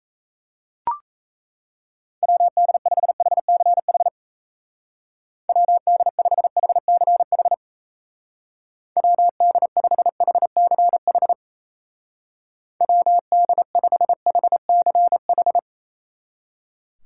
Morse code, heard as a keyed tone, about 34 wpm, 700 Hz. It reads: WD5HCH WD5HCH WD5HCH WD5HCH
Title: Speed Racing - USA 2x3 Call Signs 28wpm